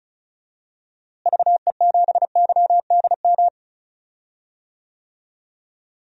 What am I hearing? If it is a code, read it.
Morse code, heard as a keyed tone, about 34 wpm, 700 Hz. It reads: VE7YDM
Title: DXCC-100 Call Signs - 6 Characters - Repeated 2x - 35wpm